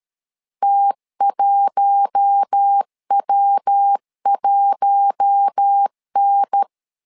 Morse code, heard as a keyed tone, about 13 wpm, 800 Hz. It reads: T1W1N
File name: LEZIONI / 4 Q1 12wpm 2028